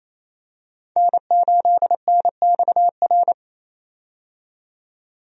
Morse code, heard as a keyed tone, about 28 wpm, 700 Hz. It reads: N8NXR
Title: USA Call Signs - 5 Characters - 28wpm